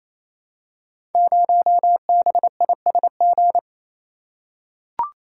Morse code, heard as a keyed tone, about 27 wpm, 700 Hz. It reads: 0BISG
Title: Code Groups - 5 Letter:Numbers - 28wpm